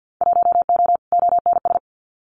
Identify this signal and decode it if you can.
Morse code, heard as a keyed tone, about 50 wpm, 700 Hz. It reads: 1O ONS
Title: Winter Field Day Exchanges 50wpm